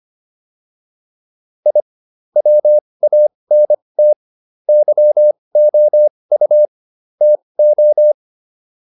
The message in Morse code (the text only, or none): I WANT YOU TO